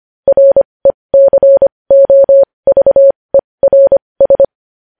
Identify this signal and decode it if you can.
Morse code, heard as a keyed tone, about 25 wpm, spaced as for 20 wpm, 550 Hz. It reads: RECOVERS
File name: podcast / News headlines-1x-25x20